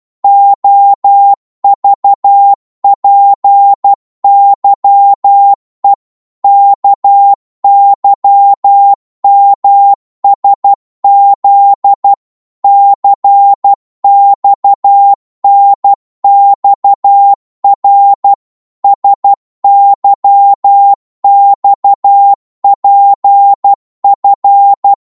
Morse code, heard as a keyed tone, about 12 wpm, 800 Hz. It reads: OVPYE KYMSZ CXNXR SYXPF
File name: Morse / ltr-12wpm